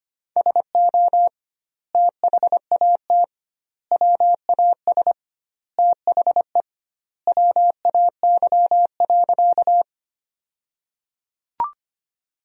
Morse code, heard as a keyed tone, about 25 wpm, 700 Hz. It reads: SO THAT WAS THE WAY.
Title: 2x Word-Spacing - Sentences from Top 100 Words - 25wpm